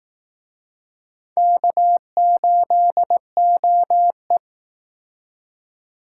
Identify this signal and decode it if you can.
Morse code, heard as a keyed tone, about 19 wpm, 700 Hz. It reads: K8OE